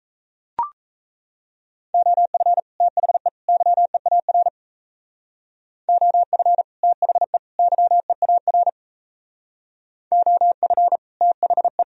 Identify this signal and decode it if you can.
Morse code, heard as a keyed tone, about 36 wpm, 700 Hz. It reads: OF THE YEAR OF THE YEAR OF THE